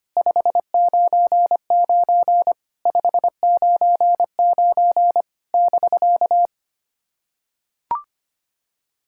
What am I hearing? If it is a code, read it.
Morse code, heard as a keyed tone, about 25 wpm, 700 Hz. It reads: 599 599 <BK>
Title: SOTA QSO Practice 25wpm